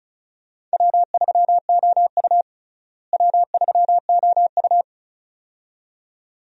Morse code, heard as a keyed tone, about 34 wpm, 700 Hz. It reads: W3OU W3OU
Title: USA Call Signs - 4 Characters - Repeated 2x - 35wpm